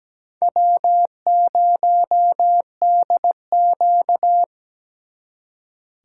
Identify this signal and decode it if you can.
Morse code, heard as a keyed tone, about 20 wpm, 700 Hz. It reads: W0DQ
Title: Speed Racing - USA 1x2 Call Signs 17wpm